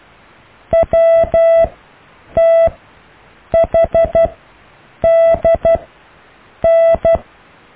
Morse code, heard as a keyed tone, about 12 wpm, spaced as for 7 wpm, 660 Hz. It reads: WTHDN
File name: cw / EKTPV+BFHNW